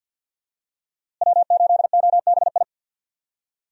Morse code, heard as a keyed tone, about 48 wpm, 700 Hz. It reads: W8OBI
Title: USA Call Signs - 5 Characters - Repeated 2x - 50wpm